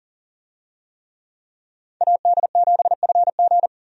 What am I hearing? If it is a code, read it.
Morse code, heard as a keyed tone, about 40 wpm, 700 Hz. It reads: AD7FG
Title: USA Call Signs - 5 Characters - 40wpm